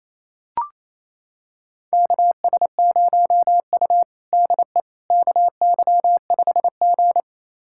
Morse code, heard as a keyed tone, about 28 wpm, 700 Hz. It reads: KS0U DE KY5G
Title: USA Callsign DE Callsign - 4 Characters - 28wpm